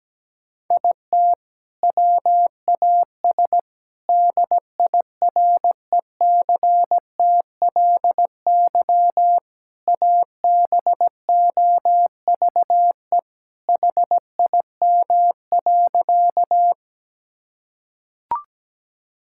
Morse code, heard as a keyed tone, about 17 wpm, 700 Hz. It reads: IT WAS DIRECTLY ABOVE HIM.
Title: Sentences from Top 1000 Words - Review Part 5 - 17wpm